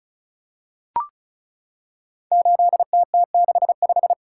Morse code, heard as a keyed tone, about 36 wpm, 700 Hz. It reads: 8TT65
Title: Code Groups - 5 Letter:Numbers - 35wpm